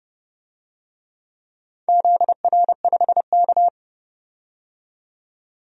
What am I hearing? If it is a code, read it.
Morse code, heard as a keyed tone, about 30 wpm, 700 Hz. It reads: ZR5K
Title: DXCC-100 Call Signs - 4 Characters - 30wpm